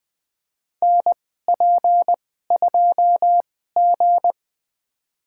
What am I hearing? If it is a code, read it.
Morse code, heard as a keyed tone, about 20 wpm, spaced as for 14 wpm, 700 Hz. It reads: NP2G
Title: USA Call Signs - 2x Character Spacing - 20wpm